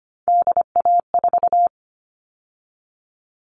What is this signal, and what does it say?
Morse code, heard as a keyed tone, about 25 wpm, 700 Hz. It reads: DA4
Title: All Cut Numbers - 3-Digit 25wpm